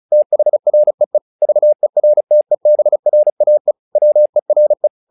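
Morse code, heard as a keyed tone, about 36 wpm, 600 Hz. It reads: THREE VERTEBRAE WERE
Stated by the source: Morse practice text "Tower Trouble" sent at 35 WPM, from the SKCC - Straight Key Century Club learning center audio files.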